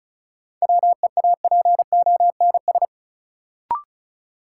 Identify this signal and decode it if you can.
Morse code, heard as a keyed tone, about 34 wpm, 700 Hz. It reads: WEAPONS
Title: Sets of 5 Words Encore - New Word Drill - 35wpm